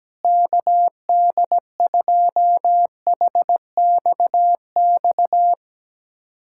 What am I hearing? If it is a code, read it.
Morse code, heard as a keyed tone, about 17 wpm, 700 Hz. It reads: KD2HXX
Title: USA Callsign DE Callsign 17wpm